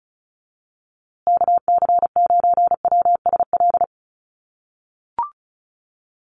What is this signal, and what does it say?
Morse code, heard as a keyed tone, about 34 wpm, 700 Hz. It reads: KC9WSL
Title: USA Call Signs Repeated 2x 35wpm